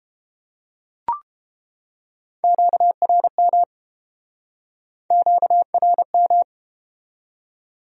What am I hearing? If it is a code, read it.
Morse code, heard as a keyed tone, about 33 wpm, 700 Hz. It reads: QRM QRM